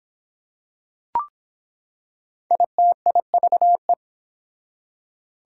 Morse code, heard as a keyed tone, about 24 wpm, 700 Hz. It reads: ITIVE